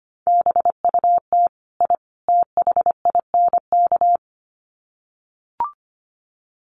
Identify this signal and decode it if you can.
Morse code, heard as a keyed tone, about 25 wpm, 700 Hz. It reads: BUT I THINK
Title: Sets of 3 Words 25wpm